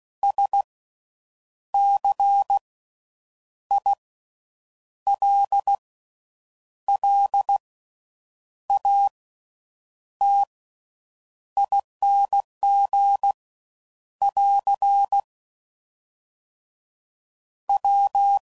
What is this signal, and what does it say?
Morse code, heard as a keyed tone, about 16 wpm, spaced as for 7 wpm, 800 Hz. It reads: SCILLATING <AR> W
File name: Suffixes / Special INGs / Slow With Faster Ings